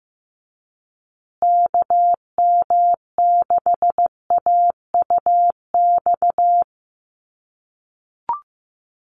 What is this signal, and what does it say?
Morse code, heard as a keyed tone, about 15 wpm, 700 Hz. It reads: KM6AUX
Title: USA Call Signs Repeated 2x 15wpm